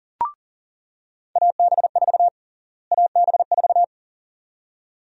Morse code, heard as a keyed tone, about 40 wpm, 700 Hz. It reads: AB4 AB4
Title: USA Call Sign Prefixes - Repeated 2x - 40wpm